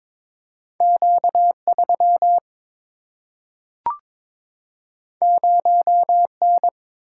Morse code, heard as a keyed tone, about 22 wpm, 700 Hz. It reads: Q 3 0 N
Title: Single-Letter Single-Number 22wpm